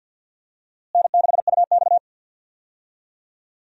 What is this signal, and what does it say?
Morse code, heard as a keyed tone, about 52 wpm, 700 Hz. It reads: N6UX
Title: USA Call Signs - 4 Characters - Repeated 2x - 50wpm